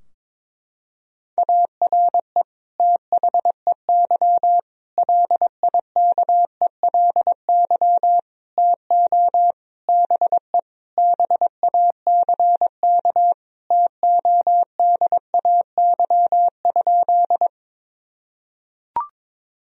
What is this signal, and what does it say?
Morse code, heard as a keyed tone, about 22 wpm, 700 Hz. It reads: ARE THEY LIKELY TO BE BACK TODAY?
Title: Sentences from Top 1000 Words - Review Part 4 - 22wpm